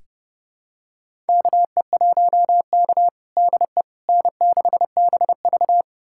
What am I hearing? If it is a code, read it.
Morse code, heard as a keyed tone, about 30 wpm, 700 Hz. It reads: KE1K DE N6BV
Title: USA Callsign DE Callsign - 4 Characters - 30wpm